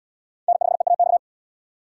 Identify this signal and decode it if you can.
Morse code, heard as a keyed tone, about 71 wpm, 700 Hz. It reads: N5EAX